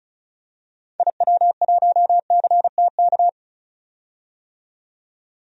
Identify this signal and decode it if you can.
Morse code, heard as a keyed tone, about 36 wpm, 700 Hz. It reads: IW1CTK
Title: DXCC-100 Call Signs - 6 Characters - 35wpm